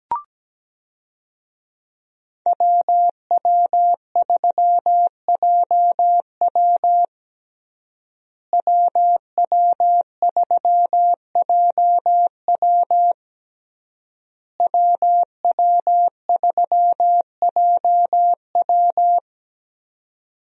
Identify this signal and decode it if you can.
Morse code, heard as a keyed tone, about 17 wpm, 700 Hz. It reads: WW3JW WW3JW WW3JW
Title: USA Call Signs - 5 Characters - Repeated 3x - 17wpm